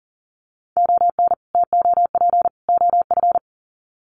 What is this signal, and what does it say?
Morse code, heard as a keyed tone, about 39 wpm, 700 Hz. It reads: ON TOP OF
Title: Sets of 3 Words 40wpm